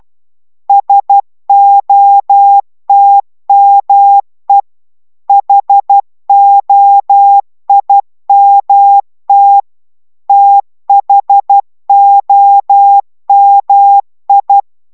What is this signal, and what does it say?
Morse code, heard as a keyed tone, about 12 wpm, 800 Hz. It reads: SOTME HOIMT THOMI